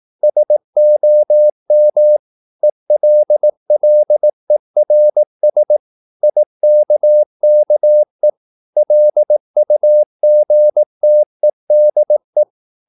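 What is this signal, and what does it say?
Morse code, heard as a keyed tone, about 18 wpm, 600 Hz. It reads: SOM ELLERS IKKE LUGTEDE
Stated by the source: Slow (QRS) Morse practice at 18 wpm, CW med Gnister.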